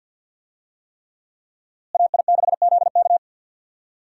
Morse code, heard as a keyed tone, about 52 wpm, 700 Hz. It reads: AI6ZK
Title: USA Call Signs - 5 Characters - Repeated 3x - 50wpm